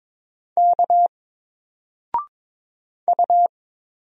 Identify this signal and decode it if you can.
Morse code, heard as a keyed tone, about 23 wpm, 700 Hz. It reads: K U
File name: Single-Letter-Number-22wpm